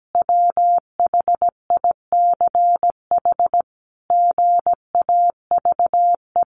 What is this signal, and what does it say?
Morse code, heard as a keyed tone, about 17 wpm, 700 Hz. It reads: WHICH GAVE
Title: Sentences from Top 1500 Words 17wpm